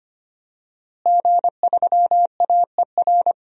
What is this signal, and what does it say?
Morse code, heard as a keyed tone, about 25 wpm, 700 Hz. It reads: G3AER
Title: DXCC-100 Call Signs - 5 Characters - Repeated 3x - 25wpm